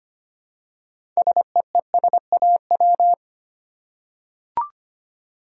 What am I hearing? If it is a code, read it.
Morse code, heard as a keyed tone, about 25 wpm, 700 Hz. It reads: SEESAW
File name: practice / 25wpm / Easy Words with Hints - 25wpm